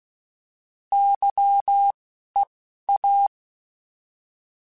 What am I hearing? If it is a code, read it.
Morse code, heard as a keyed tone, about 16 wpm, spaced as for 12 wpm, 800 Hz. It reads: YEA